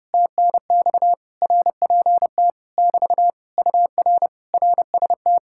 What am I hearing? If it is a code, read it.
Morse code, heard as a keyed tone, about 33 wpm, 700 Hz. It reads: TNX RPT <BT> UR RST